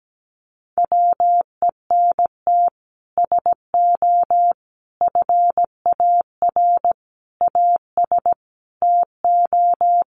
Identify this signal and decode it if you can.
Morse code, heard as a keyed tone, about 17 wpm, 700 Hz. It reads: WENT SO FAR AS TO
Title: Sets of 5 Words Encore 17wpm